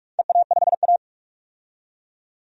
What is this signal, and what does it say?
Morse code, heard as a keyed tone, about 44 wpm, 700 Hz. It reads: EA5A
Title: DXCC-100 Call Signs - 4 Characters - 45wpm